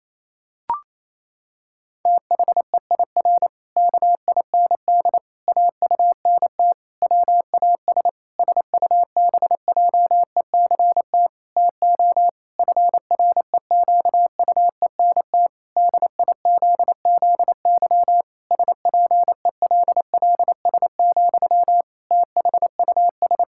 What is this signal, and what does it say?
Morse code, heard as a keyed tone, about 28 wpm, 700 Hz. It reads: THEIR KIND AUNT WAS SUBJECT TO FREQUENT DIZZY SPELLS, THUS
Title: Pangram Challenge v2 - 28wpm